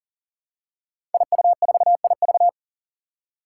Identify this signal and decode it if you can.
Morse code, heard as a keyed tone, about 40 wpm, 700 Hz. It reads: IU4IV
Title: Five Challenging Letter-Number Combinations 40wpm